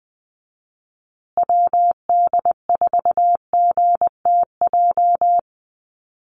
Morse code, heard as a keyed tone, about 20 wpm, 700 Hz. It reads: WD4GTJ
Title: USA Call Signs Repeated 2x 20wpm